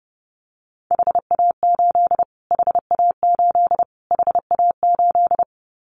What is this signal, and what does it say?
Morse code, heard as a keyed tone, about 30 wpm, 700 Hz. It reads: HA8 HA8 HA8
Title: Callsign Prefixes for Top 25 DXCC - Repeated 3x - 30wpm